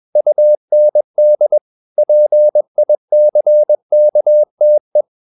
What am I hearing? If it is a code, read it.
Morse code, heard as a keyed tone, about 21 wpm, 600 Hz. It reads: UND PICKTE